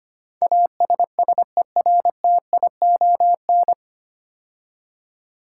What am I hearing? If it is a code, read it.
Morse code, heard as a keyed tone, about 25 wpm, 700 Hz. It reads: ASSERTION
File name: practice / 25wpm / 9 Letter Words - First 3 Letters Given - 25wpm